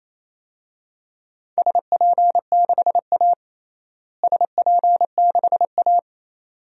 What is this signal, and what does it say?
Morse code, heard as a keyed tone, about 28 wpm, 700 Hz. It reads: SP6A SP6A
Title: DXCC-100 Call Signs - 4 Characters - Repeated 2x - 28wpm